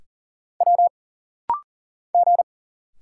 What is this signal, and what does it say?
Morse code, heard as a keyed tone, about 39 wpm, 700 Hz. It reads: W G
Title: Single Letters - Rapid-Fire 40wpm